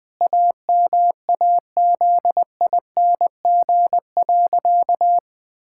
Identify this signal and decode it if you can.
Morse code, heard as a keyed tone, about 21 wpm, 700 Hz. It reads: AMAZING.
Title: Sentences with 6-Word Phrases - 20wpm